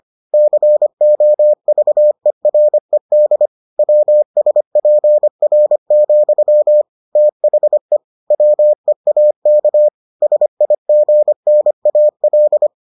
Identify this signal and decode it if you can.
Morse code, heard as a keyed tone, about 25 wpm, 600 Hz. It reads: COVERED WSPR, THE WEAK SIGNAL